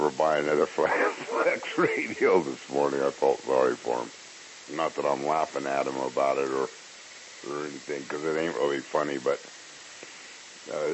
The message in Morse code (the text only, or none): none